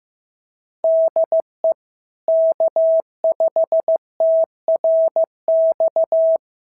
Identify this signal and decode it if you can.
Morse code, heard as a keyed tone, about 15 wpm, 650 Hz. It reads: DE K5TRX
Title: SOTA QSO Practice 15wpm